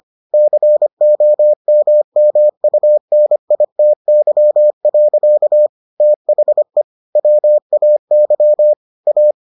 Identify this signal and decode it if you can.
Morse code, heard as a keyed tone, about 25 wpm, 600 Hz. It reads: COMMUNITY. THE WAY A